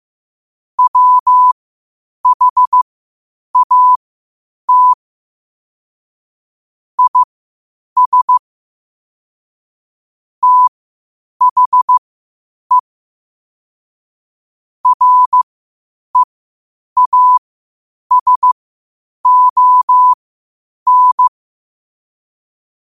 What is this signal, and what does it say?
Morse code, heard as a keyed tone, about 15 wpm, spaced as for 8 wpm, 1000 Hz. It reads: WHAT IS THE REASON
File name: qotdfiles / 8wpm